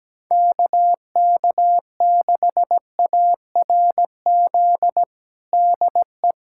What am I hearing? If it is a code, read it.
Morse code, heard as a keyed tone, about 17 wpm, 700 Hz. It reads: KK6ARZ DE
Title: USA Callsign DE Callsign 17wpm